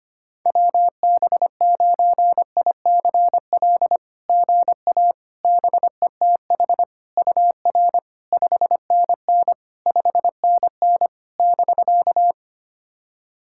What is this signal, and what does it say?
Morse code, heard as a keyed tone, about 25 wpm, 690 Hz. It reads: WB9ICL GA BETH UR 5NN 5NN <BK>
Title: SOTA QSO Practice 25wpm